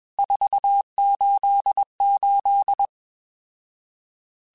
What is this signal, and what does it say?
Morse code, heard as a keyed tone, about 21 wpm, 800 Hz. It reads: 488
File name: RSTs / RST-mp3s / Various RST @20